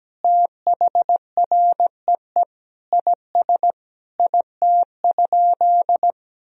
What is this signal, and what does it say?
Morse code, heard as a keyed tone, about 17 wpm, 700 Hz. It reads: THREE IS IT?
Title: Sentences from Top 1000 Words - Review Part 1 - 17wpm